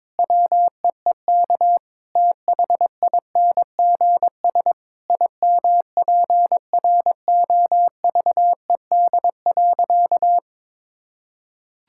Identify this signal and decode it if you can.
Morse code, heard as a keyed tone, about 21 wpm, 700 Hz. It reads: WEEK THINGS IMPROVED.
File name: practice / 22wpm / Sentences with 4-Word Phrases - 22wpm